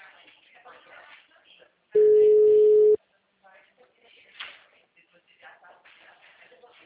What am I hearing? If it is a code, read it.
Ringback tone (Europe, 425 Hz), with speech.